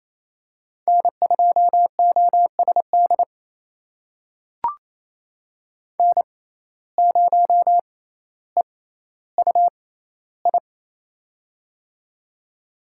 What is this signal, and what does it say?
Morse code, heard as a keyed tone, about 28 wpm, spaced as for 14 wpm, 700 Hz. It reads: N2OSD N0EUI
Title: USA Call Signs - 5 Characters - 6x Character Spacing - 28wpm